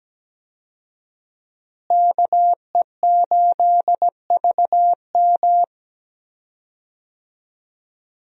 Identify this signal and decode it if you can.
Morse code, heard as a keyed tone, about 17 wpm, 700 Hz. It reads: KE8VM
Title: USA Call Signs - 5 Characters - Repeated 3x - 17wpm